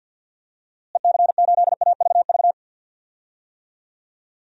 Morse code, heard as a keyed tone, about 51 wpm, 700 Hz. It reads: EC8AVV
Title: DXCC-100 Call Signs - 6 Characters - Repeated 3x - 50wpm